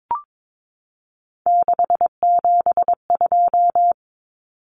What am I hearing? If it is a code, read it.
Morse code, heard as a keyed tone, about 22 wpm, 700 Hz. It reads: 672
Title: Numbers - 3-Digit 22wpm